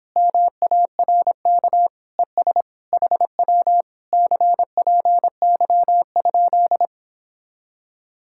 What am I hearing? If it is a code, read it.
Morse code, heard as a keyed tone, about 31 wpm, 700 Hz. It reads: MARK ES HW CPY?